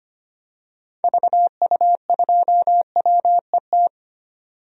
Morse code, heard as a keyed tone, about 25 wpm, 700 Hz. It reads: VU2WET
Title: DXCC-100 Call Signs - 6 Characters - 25wpm